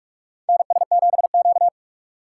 Morse code, heard as a keyed tone, about 47 wpm, 700 Hz. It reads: NS7Q
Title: USA Call Signs - 4 Characters - 3x Character Spacing - 45wpm